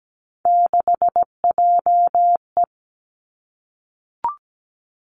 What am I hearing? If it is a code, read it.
Morse code, heard as a keyed tone, about 17 wpm, 700 Hz. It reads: 6JE
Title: Three Challenging Letter-Number Combinations 17wpm